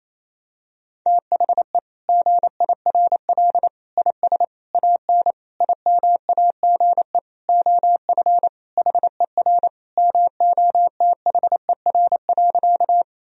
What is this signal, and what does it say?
Morse code, heard as a keyed tone, about 28 wpm, 700 Hz. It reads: THE GIRL IS AN IMAGE OF HER MOTHER.
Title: Sentences from Top 1000 Words - Review Part 4 - 28wpm